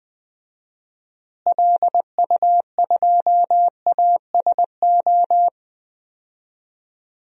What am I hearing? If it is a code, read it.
Morse code, heard as a keyed tone, about 20 wpm, 700 Hz. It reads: LU2ASO
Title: DXCC-100 Call Signs - 6 Characters - Repeated 3x - 20wpm